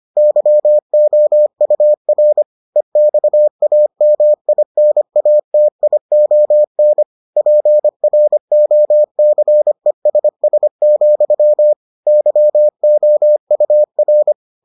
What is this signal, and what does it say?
Morse code, heard as a keyed tone, about 25 wpm, 600 Hz. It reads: YOUR EXAMINATION PROCESS, YOUR